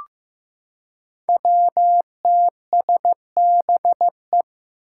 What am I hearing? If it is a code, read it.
Morse code, heard as a keyed tone, about 15 wpm, 700 Hz. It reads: WTSBE
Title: Five Challenging Letter-Number Combinations 15wpm